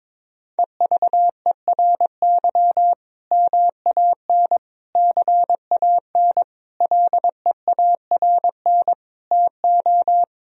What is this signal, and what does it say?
Morse code, heard as a keyed tone, about 22 wpm, 700 Hz. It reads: EVERY MAN CAN LEARN TO
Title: Sentences from Top 1000 Words - Review Part 1 - 22wpm